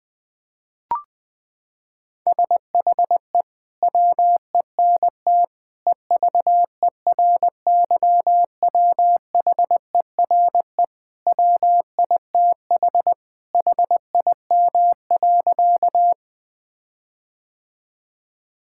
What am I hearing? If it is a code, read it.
Morse code, heard as a keyed tone, about 20 wpm, 700 Hz. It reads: SHE WENT EVERYWHERE WITH HIM.
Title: Sentences from Top 1600 Words 20wpm